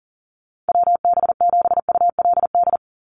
Morse code, heard as a keyed tone, about 41 wpm, 700 Hz. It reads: WB7ULD
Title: NA SOTA Chaser Callsigns 40wpm